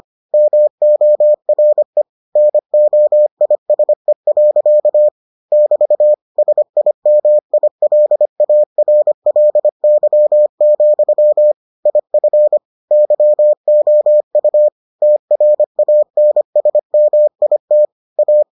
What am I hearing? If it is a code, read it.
Morse code, heard as a keyed tone, about 25 wpm, 600 Hz. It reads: MORE NOISE. <BT> SIMILARLY, IF YOU TRANSMIT A